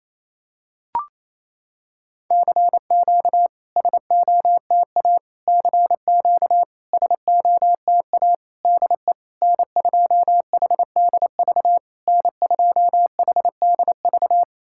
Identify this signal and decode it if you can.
Morse code, heard as a keyed tone, about 28 wpm, 700 Hz. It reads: CQ SOTA CQ SOTA DE N2HDV N2HDV
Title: SOTA QSO Practice 28wpm